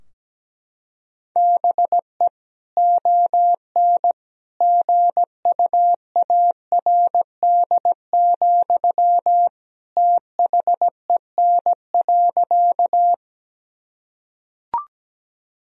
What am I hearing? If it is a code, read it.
Morse code, heard as a keyed tone, about 17 wpm, 700 Hz. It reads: BE ON GUARD, THEN.
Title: Sentences from Top 1000 Words - Review Part 4 - 17wpm